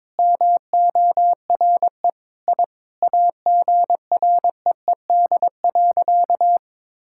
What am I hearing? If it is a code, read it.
Morse code, heard as a keyed tone, about 21 wpm, 700 Hz. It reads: MORE I AGREED.
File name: Sentences with 9-Word Phrases - 22wpm